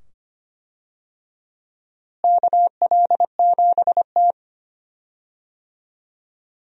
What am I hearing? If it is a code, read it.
Morse code, heard as a keyed tone, about 25 wpm, 700 Hz. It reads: KL7T
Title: USA Call Signs - 4 Characters - Repeated 2x - 25wpm